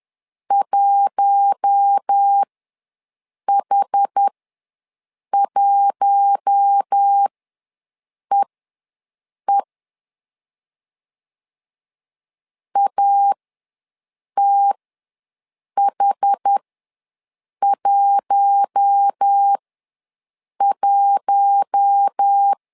Morse code, heard as a keyed tone, about 10 wpm, spaced as for 6 wpm, 800 Hz. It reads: 1H1EE ATH11
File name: LEZIONI / 4 Q1 ws 10wpm 2538